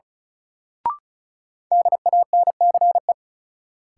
Morse code, heard as a keyed tone, about 36 wpm, 700 Hz. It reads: DANCE
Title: Sentences from Top 700 Words - New Word Drill - 35wpm